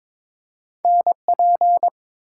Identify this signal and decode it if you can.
Morse code, heard as a keyed tone, about 22 wpm, 700 Hz. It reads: NP
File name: practice / 22wpm / Two Letter-Number Combinations - 22wpm